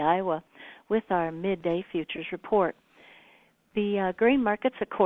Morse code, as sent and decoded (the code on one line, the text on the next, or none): none